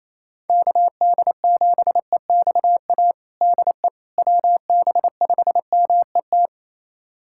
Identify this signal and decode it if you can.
Morse code, heard as a keyed tone, about 28 wpm, 700 Hz. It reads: KD7EXA DE WB5MET
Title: USA Callsign DE Callsign - 6 Characters - 28wpm